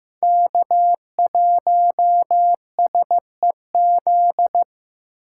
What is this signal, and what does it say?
Morse code, heard as a keyed tone, about 15 wpm, 700 Hz. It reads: K1SEZ
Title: USA Call Signs - 5 Characters - 6x Character Spacing - 15wpm